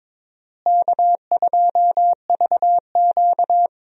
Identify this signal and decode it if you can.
Morse code, heard as a keyed tone, about 22 wpm, 700 Hz. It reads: K2VQ
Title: USA Call Signs - 4 Characters - Repeated 2x - 22wpm